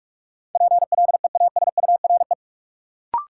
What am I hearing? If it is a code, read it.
Morse code, heard as a keyed tone, about 46 wpm, 700 Hz. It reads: PLEASURE
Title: Sentences from Top 1600 Words - New Word Drill - 45wpm